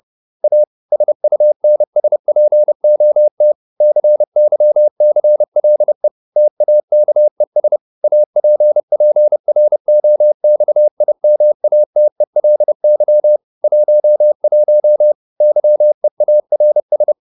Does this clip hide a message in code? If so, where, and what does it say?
Yes: Morse code from 0.4 to 17.1 s, heard as a keyed tone, about 30 wpm, 600 Hz. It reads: A SUNSPOT CYCLE TAKES APPROXIMATELY 11 YEARS